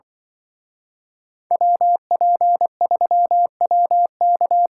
Morse code, heard as a keyed tone, about 24 wpm, 700 Hz. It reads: WP3WK
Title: Speed Racing - USA 2x2 Call Signs 20wpm